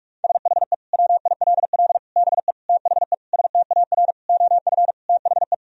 Morse code, heard as a keyed tone, about 46 wpm, 700 Hz. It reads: SHE WILL BE THE STAR OF THE
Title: Sentences from Top 1000 Words - Review Part 3 - 45wpm